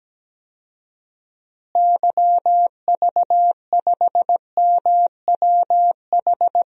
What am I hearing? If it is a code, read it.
Morse code, heard as a keyed tone, about 17 wpm, 700 Hz. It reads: YV5MWH
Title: DXCC-100 Call Signs - 6 Characters - Repeated 2x - 17wpm